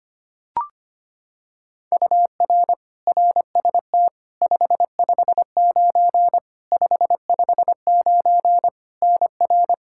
Morse code, heard as a keyed tone, about 25 wpm, 700 Hz. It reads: UR RST 559 559 NR
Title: Practice QSO - Exchange 1 - Beginning - 25wpm